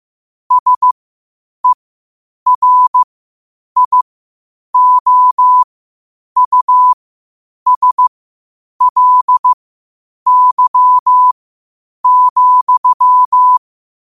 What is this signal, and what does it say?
Morse code, heard as a keyed tone, about 15 wpm, spaced as for 8 wpm, 1000 Hz. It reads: SERIOUSLY,
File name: qotdfiles / 8wpm